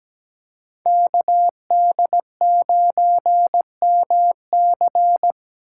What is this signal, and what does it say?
Morse code, heard as a keyed tone, about 17 wpm, 700 Hz. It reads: KD9MC
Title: USA Call Signs - 5 Characters - 3x Character Spacing - 17wpm